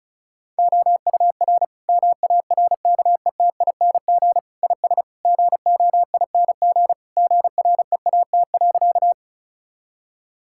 Morse code, heard as a keyed tone, about 36 wpm, 700 Hz. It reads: OUR MARKETING IS GOING GREAT.
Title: Sentences from Top 1000 Words - Review Part 5 - 35wpm